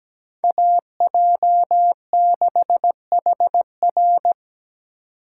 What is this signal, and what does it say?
Morse code, heard as a keyed tone, about 20 wpm, 700 Hz. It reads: AJ6HR
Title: Speed Racing - USA 2x2 Call Signs 17wpm